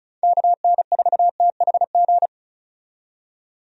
Morse code, heard as a keyed tone, about 36 wpm, 700 Hz. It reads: KN4THG
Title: USA Call Signs - 6 Characters - 35wpm